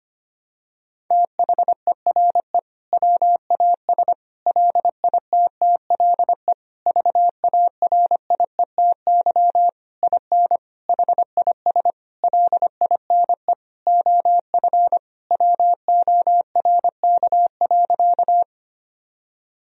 Morse code, heard as a keyed tone, about 25 wpm, 700 Hz. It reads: THERE WAS LITTLE VARIETY IN HIS LINE OF WORK.
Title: Sentences from Top 1000 Words - Review Part 4 - 25wpm